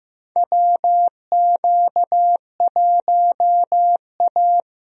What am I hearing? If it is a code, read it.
Morse code, heard as a keyed tone, about 15 wpm, 700 Hz. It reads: WQ1A
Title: USA Call Signs - 4 Characters - Repeated 2x - 15wpm